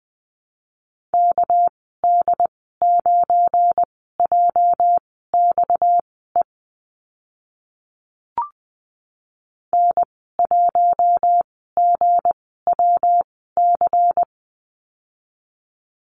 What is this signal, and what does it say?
Morse code, heard as a keyed tone, about 20 wpm, spaced as for 14 wpm, 700 Hz. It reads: KD9JXE N1GWC
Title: 238 Lesson 38g - Letter q - Call Signs 20wpm